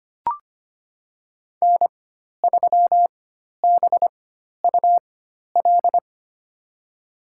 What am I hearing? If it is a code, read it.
Morse code, heard as a keyed tone, about 25 wpm, spaced as for 12 wpm, 700 Hz. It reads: N3BUL
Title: USA Call Signs - 5 Characters - 4x Character Spacing - 25wpm